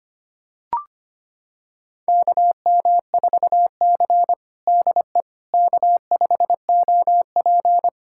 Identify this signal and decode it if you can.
Morse code, heard as a keyed tone, about 25 wpm, 700 Hz. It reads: KM4C DE K5OP
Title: USA Callsign DE Callsign - 4 Characters - 25wpm